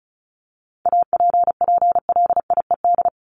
Morse code, heard as a keyed tone, about 34 wpm, 700 Hz. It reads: APPLIED
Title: ED Suffix - Top 100 Words with Base Words 35wpm